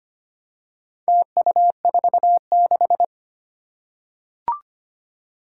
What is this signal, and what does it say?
Morse code, heard as a keyed tone, about 25 wpm, 700 Hz. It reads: TU46
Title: Four Challenging Letter-Number Combinations 25wpm